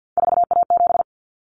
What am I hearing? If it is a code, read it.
Morse code, heard as a keyed tone, about 50 wpm, 700 Hz. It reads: 4A7
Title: Common Cut Numbers - 3-Digit 50wpm